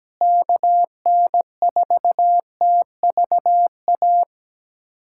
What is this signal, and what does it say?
Morse code, heard as a keyed tone, about 17 wpm, 700 Hz. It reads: KN4TVA
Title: USA Callsign DE Callsign 17wpm